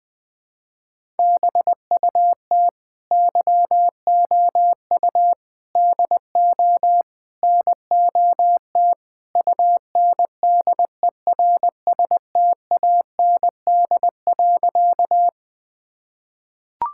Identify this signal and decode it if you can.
Morse code, heard as a keyed tone, about 20 wpm, 700 Hz. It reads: BUT YOU DO NOT UNDERSTAND.
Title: Sentences with 4-Word Phrases - 20wpm